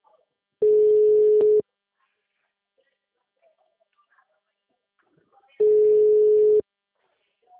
Ringback tone (Europe, 425 Hz).